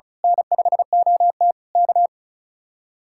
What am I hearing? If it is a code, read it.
Morse code, heard as a keyed tone, about 32 wpm, 700 Hz. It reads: N5OT K
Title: Speed-Racing Common QSO Phrases Encore - 35wpm